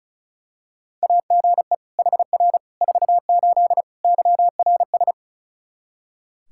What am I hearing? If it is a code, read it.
Morse code, heard as a keyed tone, about 36 wpm, 700 Hz. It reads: AGE HR 48 YRS
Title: Practice QSO - Exchange 3 - More Info - 35wpm